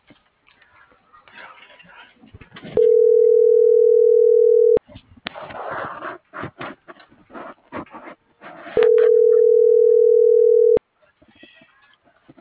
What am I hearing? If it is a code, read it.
Ringback tone (North America), 440 + 480 Hz.